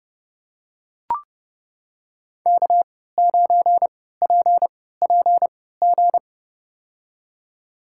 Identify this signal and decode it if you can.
Morse code, heard as a keyed tone, about 31 wpm, spaced as for 17 wpm, 700 Hz. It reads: K9PPG
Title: USA Call Signs - 3x Character Spacing - 30wpm